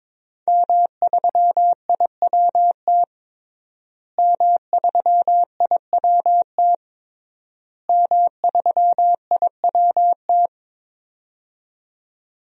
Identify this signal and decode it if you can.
Morse code, heard as a keyed tone, about 22 wpm, 700 Hz. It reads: M3IWT M3IWT M3IWT
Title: DXCC-100 Call Signs - 5 Characters - Repeated 3x - 22wpm